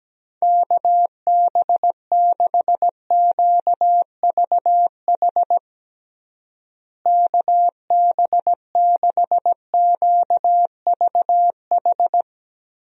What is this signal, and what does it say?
Morse code, heard as a keyed tone, about 17 wpm, 700 Hz. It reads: KB6QVH KB6QVH
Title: USA Call Signs - 6 Characters - Repeated 3x - 17wpm